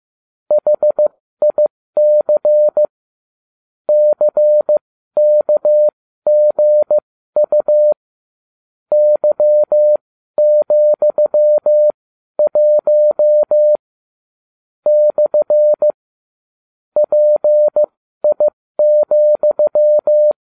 Morse code, heard as a keyed tone, about 15 wpm, spaced as for 12 wpm, 600 Hz. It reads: HIC CKGU Y,1 / PI,